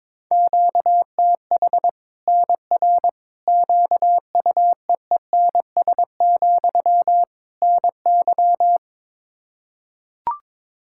Morse code, heard as a keyed tone, about 22 wpm, 700 Hz. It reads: QTH NR QUEENS, NY
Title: Practice QSO - Exchange 1 - Beginning - 22wpm